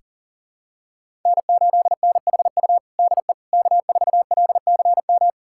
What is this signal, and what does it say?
Morse code, heard as a keyed tone, about 40 wpm, 700 Hz. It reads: N8NHU DE K4LCM
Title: USA Callsign DE Callsign - 5 Characters - 40wpm